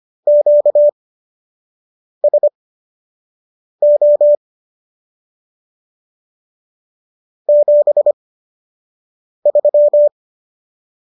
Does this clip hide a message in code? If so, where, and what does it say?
Yes: Morse code from 0.3 to 10.1 s, heard as a keyed tone, about 25 wpm, spaced as for 6 wpm, 590 Hz. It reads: QSO 73